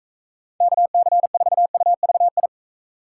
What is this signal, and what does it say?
Morse code, heard as a keyed tone, about 42 wpm, 700 Hz. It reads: KC4UVI